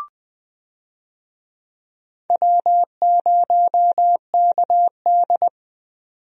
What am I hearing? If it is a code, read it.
Morse code, heard as a keyed tone, about 21 wpm, 700 Hz. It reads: W0KD
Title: USA Call Signs - 4 Characters - 20wpm